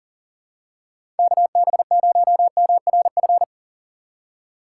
Morse code, heard as a keyed tone, about 40 wpm, 700 Hz. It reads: KB0MRF
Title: USA Call Signs - 2x Character Spacing - 40wpm